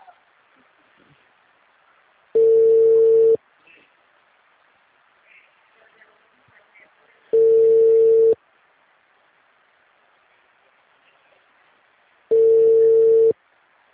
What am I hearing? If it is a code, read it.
Ringback tone (China, 450 Hz), with speech.